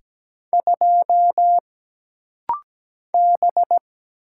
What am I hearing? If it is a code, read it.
Morse code, heard as a keyed tone, about 17 wpm, 700 Hz. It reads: 2 B
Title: Single Letter-Number - Rapid-Fire 17wpm